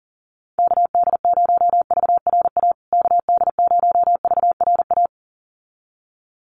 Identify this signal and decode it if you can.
Morse code, heard as a keyed tone, about 40 wpm, 700 Hz. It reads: KD0VRA KD0VRA
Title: USA Call Signs Repeated 2x 40wpm